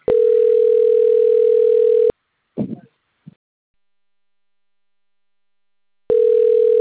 Ringback tone (North America), 440 + 480 Hz.
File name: out- - - - - .35295